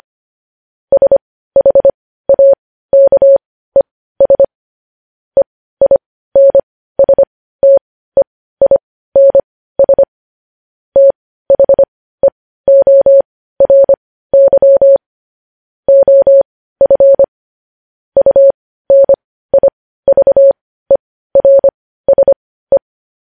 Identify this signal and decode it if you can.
Morse code, heard as a keyed tone, about 25 wpm, spaced as for 15 wpm, 550 Hz. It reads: SHAKES EINSTEINS THEORY OF UNIVERSE
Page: news-headlines-in-morse-code-at-15-wpm